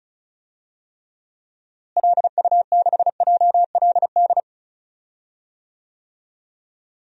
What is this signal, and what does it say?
Morse code, heard as a keyed tone, about 34 wpm, 700 Hz. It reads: LU6JLD
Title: DXCC-100 Call Signs - 6 Characters - Repeated 3x - 35wpm